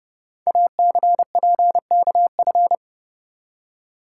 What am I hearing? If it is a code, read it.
Morse code, heard as a keyed tone, about 30 wpm, 700 Hz. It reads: ACPKF
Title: Code Groups - 5 Letter:Numbers - 30wpm